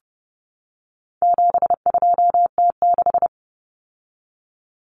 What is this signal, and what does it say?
Morse code, heard as a keyed tone, about 30 wpm, 700 Hz. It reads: 72T6
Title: Common Cut Numbers - 4-Digit 30wpm